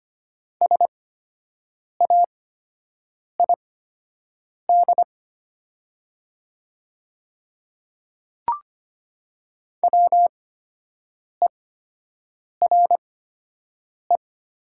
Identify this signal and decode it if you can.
Morse code, heard as a keyed tone, about 25 wpm, spaced as for 7 wpm, 700 Hz. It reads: SAID WERE
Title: 8x Character Spacing - Top 200 Words - 25wpm